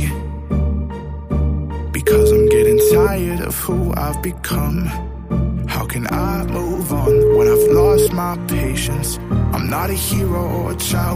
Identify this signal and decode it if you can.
Ringback tone (Europe, 425 Hz), with speech.